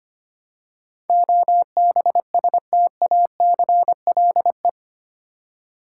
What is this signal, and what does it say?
Morse code, heard as a keyed tone, about 25 wpm, 700 Hz. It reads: OBSTACLE
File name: 8 Letter Words - First 3 Letters Given - 25wpm